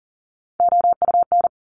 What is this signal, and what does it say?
Morse code, heard as a keyed tone, about 42 wpm, 700 Hz. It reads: OUN
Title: Top Three Letter Combinations in Words 40wpm